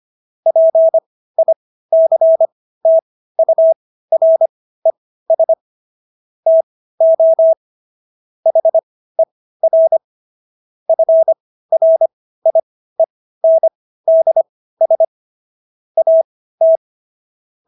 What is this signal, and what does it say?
Morse code, heard as a keyed tone, about 25 wpm, spaced as for 15 wpm, 650 Hz. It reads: PICTURES TO HER FRIENDS AT